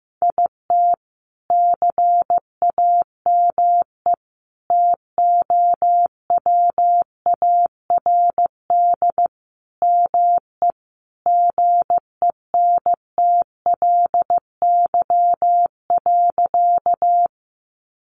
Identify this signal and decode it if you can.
Morse code, heard as a keyed tone, about 15 wpm, 700 Hz. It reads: IT CAME TOWARD ME GENTLY.
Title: Sentences from Top 1500 Words 15wpm